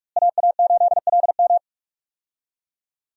Morse code, heard as a keyed tone, about 46 wpm, 700 Hz. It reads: AA8LM
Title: USA Call Signs - 5 Characters - Repeated 2x - 45wpm